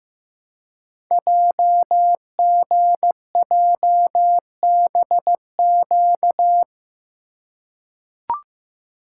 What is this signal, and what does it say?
Morse code, heard as a keyed tone, about 15 wpm, 700 Hz. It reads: JGJBQ
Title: Code Groups - 5 Letters - 15wpm